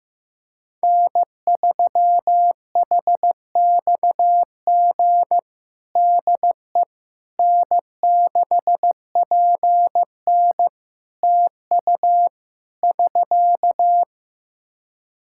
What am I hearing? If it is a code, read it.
Morse code, heard as a keyed tone, about 15 wpm, 700 Hz. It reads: N3HXG DE N6PN TU <SK>
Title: K1USN SST QSO Practice 15wpm